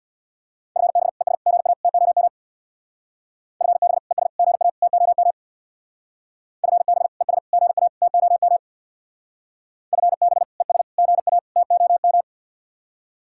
Morse code, heard as a keyed tone, about 62 wpm, 700 Hz. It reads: FB ES GA TOM FB ES GA TOM FB ES GA TOM FB ES GA TOM
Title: Speed-Racing Common QSO Phrases Encore - 50wpm